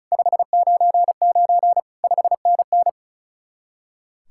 Morse code, heard as a keyed tone, about 42 wpm, 700 Hz. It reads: 599 5NN
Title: Speed-Racing Common QSO Phrases Encore - 35wpm